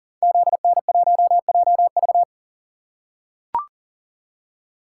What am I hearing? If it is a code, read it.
Morse code, heard as a keyed tone, about 40 wpm, 700 Hz. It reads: ZN1JV
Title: Code Groups - 5 Letter:Numbers - 40wpm